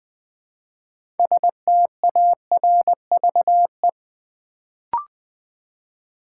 Morse code, heard as a keyed tone, about 20 wpm, 700 Hz. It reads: STARVE
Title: Build A Word - 3 Letters At A Time - 6-Letter Words - 20wpm